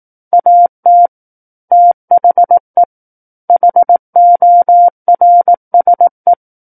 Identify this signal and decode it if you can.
Morse code, heard as a keyed tone, about 18 wpm, 700 Hz. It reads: AT THE HORSE
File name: Book One / 09-Chapter 9 - 18 WPM